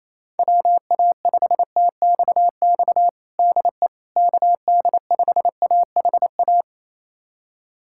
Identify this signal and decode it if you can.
Morse code, heard as a keyed tone, about 28 wpm, 700 Hz. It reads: WA5TXX DE KD5AHA
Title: USA Callsign DE Callsign - 6 Characters - 28wpm